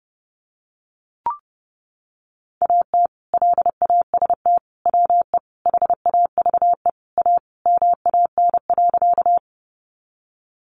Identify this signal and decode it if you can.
Morse code, heard as a keyed tone, about 30 wpm, 700 Hz. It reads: AT LAST WE HAVE A MAN.
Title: Sentences from Top 300 Words 30wpm